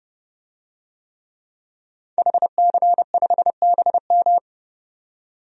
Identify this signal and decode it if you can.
Morse code, heard as a keyed tone, about 30 wpm, 700 Hz. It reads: HC5BM
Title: DXCC-100 Call Signs - 5 Characters - 30wpm